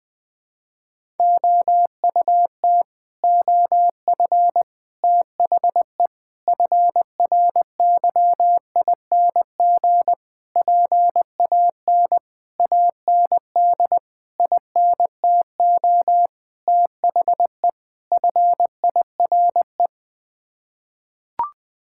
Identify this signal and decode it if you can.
Morse code, heard as a keyed tone, about 20 wpm, 700 Hz. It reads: OUT OF THE FRYING PAN AND INTO THE FIRE
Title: Sets of 9 Words - 20wpm